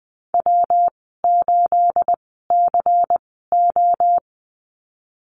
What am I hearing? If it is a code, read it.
Morse code, heard as a keyed tone, about 20 wpm, spaced as for 15 wpm, 700 Hz. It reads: W8CO
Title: 219 Lesson 35g - Number 0 - Call Signs 20wpm